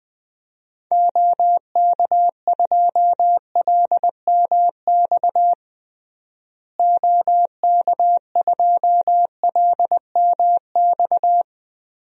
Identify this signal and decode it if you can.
Morse code, heard as a keyed tone, about 20 wpm, 700 Hz. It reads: OK2LMX OK2LMX
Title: DXCC-100 Call Signs - 6 Characters - Repeated 3x - 20wpm